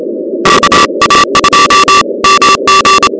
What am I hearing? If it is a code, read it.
Morse code, heard as a keyed tone, about 27 wpm, 1160 Hz, with noise. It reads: KA2MG
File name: contests / 2019-cw-wpx-kt5j-k5pi- -2330 0-16